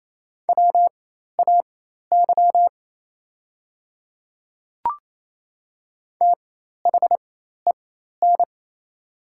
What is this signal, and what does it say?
Morse code, heard as a keyed tone, about 28 wpm, spaced as for 13 wpm, 690 Hz. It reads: WAY THEN
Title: 4x Character Spacing - Top 100 Words - 28wpm